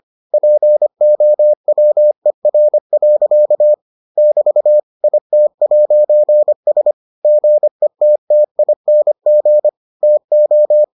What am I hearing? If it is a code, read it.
Morse code, heard as a keyed tone, about 25 wpm, 600 Hz. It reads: POWER. <BT> IT'S GETTING TO